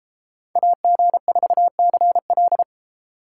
Morse code, heard as a keyed tone, about 32 wpm, 700 Hz. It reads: AG4CL